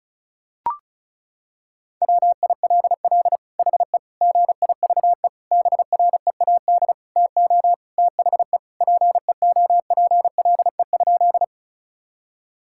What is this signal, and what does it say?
Morse code, heard as a keyed tone, about 34 wpm, 700 Hz. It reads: WILL HE GIVE BREAD TO THE PEOPLE?
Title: Sentences from Top 1600 Words 35wpm